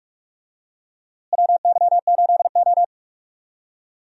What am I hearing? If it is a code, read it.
Morse code, heard as a keyed tone, about 41 wpm, 700 Hz. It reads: WY8O
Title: USA Call Signs - 4 Characters - Repeated 3x - 45wpm